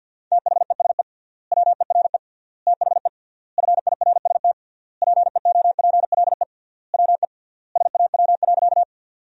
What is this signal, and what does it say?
Morse code, heard as a keyed tone, about 50 wpm, 700 Hz. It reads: THESE WERE THE FIRST PEOPLE WE SAW.
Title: 3x Word-Spacing - Sentences from Top 300 Words - 50wpm